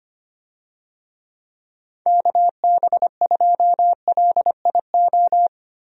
Morse code, heard as a keyed tone, about 25 wpm, 700 Hz. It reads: KB2LIO
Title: USA Call Signs - 6 Characters - 25wpm